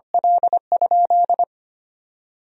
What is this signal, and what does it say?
Morse code, heard as a keyed tone, about 25 wpm, 700 Hz. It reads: L?
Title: 2x Word-Spacing - Sentences from Top 200 Words - 25wpm